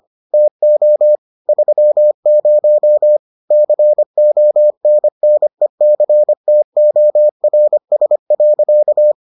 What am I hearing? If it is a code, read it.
Morse code, heard as a keyed tone, about 25 wpm, 600 Hz. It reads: TO 30 CONNECTORS.